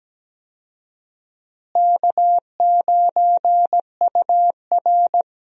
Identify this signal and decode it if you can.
Morse code, heard as a keyed tone, about 17 wpm, 700 Hz. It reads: K9UR
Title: USA Call Signs - 4 Characters - Repeated 2x - 17wpm